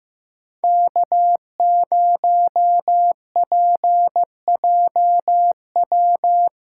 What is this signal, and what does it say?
Morse code, heard as a keyed tone, about 15 wpm, 700 Hz. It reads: K0PJW
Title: USA Callsign DE Callsign - 5 Characters - 15wpm